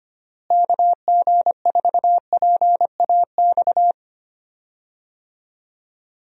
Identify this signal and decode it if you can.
Morse code, heard as a keyed tone, about 25 wpm, 700 Hz. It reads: KG4PAX
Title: USA Call Signs - 6 Characters - Repeated 2x - 25wpm